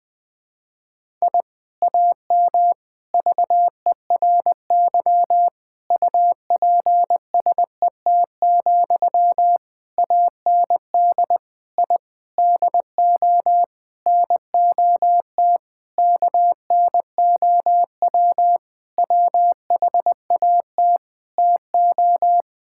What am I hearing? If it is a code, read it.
Morse code, heard as a keyed tone, about 20 wpm, 700 Hz. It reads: I AM VERY UPSET, AND I DO NOT KNOW WHAT TO
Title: Sentences from Top 1000 Words - Review Part 5 - 20wpm